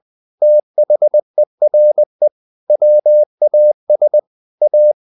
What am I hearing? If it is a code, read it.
Morse code, heard as a keyed tone, about 20 wpm, 600 Hz. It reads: THERE WAS A
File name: the old man and the sea part 2 20wpm